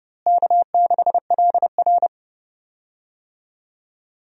Morse code, heard as a keyed tone, about 31 wpm, 700 Hz. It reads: K6LR
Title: USA Call Signs - 4 Characters - Repeated 3x - 30wpm